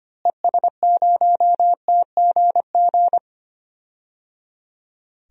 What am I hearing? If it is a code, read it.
Morse code, heard as a keyed tone, about 25 wpm, 700 Hz. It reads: ES0TGG
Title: DXCC-100 Call Signs - 6 Characters - Repeated 3x - 25wpm